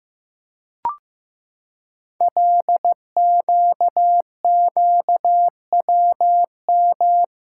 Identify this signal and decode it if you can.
Morse code, heard as a keyed tone, about 15 wpm, 700 Hz. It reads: LQQWM
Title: Code Groups - 5 Letters - 15wpm